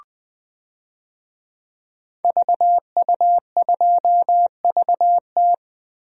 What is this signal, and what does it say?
Morse code, heard as a keyed tone, about 20 wpm, 700 Hz. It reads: VU2VT
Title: DXCC-100 Call Signs - 5 Characters - 20wpm